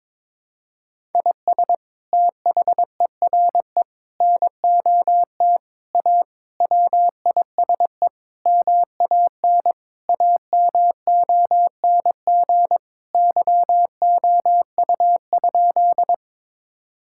Morse code, heard as a keyed tone, about 22 wpm, 700 Hz. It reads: IS THERE NOT A WISE MAN AMONG YOU?
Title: Sentences from Top 1600 Words 22wpm